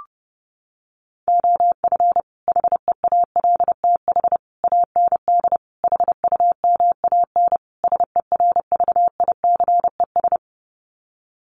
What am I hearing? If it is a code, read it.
Morse code, heard as a keyed tone, about 30 wpm, 700 Hz. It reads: OF HEALTH AND HUMAN SERVICES
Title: Sets of 5 Words 30wpm